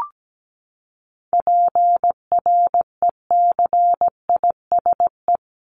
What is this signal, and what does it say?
Morse code, heard as a keyed tone, about 17 wpm, 700 Hz. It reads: PRECISE
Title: LY Suffix - Top 100 Words with Base Words 17wpm